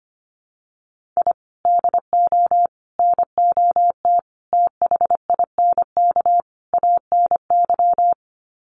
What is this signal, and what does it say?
Morse code, heard as a keyed tone, about 25 wpm, 700 Hz. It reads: I DO NOT THINK ANY
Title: Sets of 5 Words Encore 25wpm